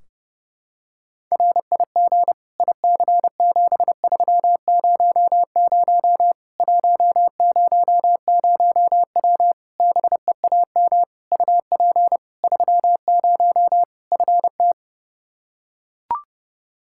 Morse code, heard as a keyed tone, about 30 wpm, 700 Hz. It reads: RIG IC7300 100W BEAM UP 30 FT
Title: Practice QSO - Exchange 2 - Info - 30wpm